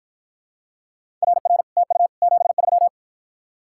Morse code, heard as a keyed tone, about 54 wpm, 700 Hz. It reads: RR TU 73